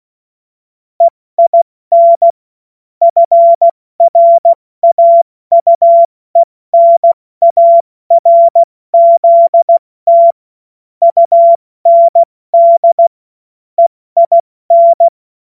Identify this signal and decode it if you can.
Morse code, heard as a keyed tone, about 16 wpm, spaced as for 14 wpm, 680 Hz. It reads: EIN FRAUENARZT UND EIN
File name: Morsekurs / Witze / Witz 16 April 2024 70 F80